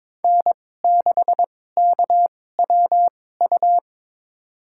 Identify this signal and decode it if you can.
Morse code, heard as a keyed tone, about 22 wpm, spaced as for 16 wpm, 700 Hz. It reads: N6KWU
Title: USA Call Signs - 2x Character Spacing - 22wpm